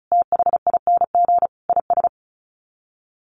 Morse code, heard as a keyed tone, about 34 wpm, 700 Hz. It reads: THING IS
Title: Sets of 5 Words Encore 35wpm